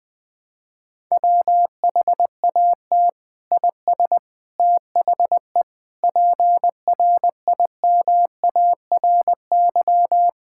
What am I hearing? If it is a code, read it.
Morse code, heard as a keyed tone, about 20 wpm, 700 Hz. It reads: WHAT IS THE PRIMARY
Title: Sentences from Top 1000 Words - Review Part 5 - 20wpm